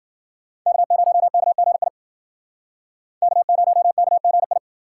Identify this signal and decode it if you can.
Morse code, heard as a keyed tone, about 51 wpm, 700 Hz. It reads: K0KGI K0KGI